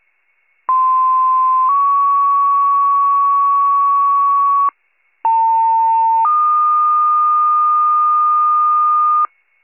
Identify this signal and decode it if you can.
Two-tone page, heard as a one-second tone, then a three-second tone, about 1040 then 1120 Hz, about 900 then 1220 Hz. It is heard from 0.7 to 9.3 s.